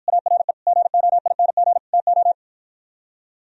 Morse code, heard as a keyed tone, about 52 wpm, 700 Hz. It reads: ARE GOING TO